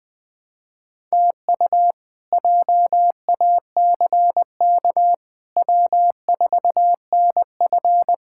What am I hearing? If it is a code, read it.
Morse code, heard as a keyed tone, about 20 wpm, 700 Hz. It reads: TU JACK W4NF
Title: K1USN SST QSO Practice - 20wpm